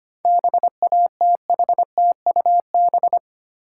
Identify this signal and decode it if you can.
Morse code, heard as a keyed tone, about 31 wpm, 700 Hz. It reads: BATHTUB
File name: Easy Words - Speed-Racing with Hints - 25wpm